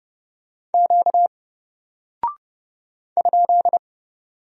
Morse code, heard as a keyed tone, about 30 wpm, 700 Hz. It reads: Q ?
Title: Single Character (Letter:Number:Punctuation:Prosign) Alt 30wpm